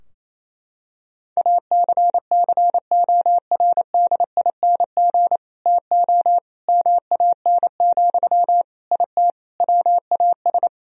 Morse code, heard as a keyed tone, about 28 wpm, 700 Hz. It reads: ACCORDING TO MAN, IT WAS
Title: Sentences from Top 1000 Words - Review Part 4 - 28wpm